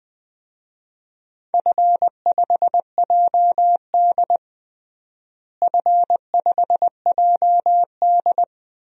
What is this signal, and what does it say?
Morse code, heard as a keyed tone, about 21 wpm, 700 Hz. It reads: F5JD F5JD
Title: DXCC-100 Call Signs - 4 Characters - Repeated 3x - 20wpm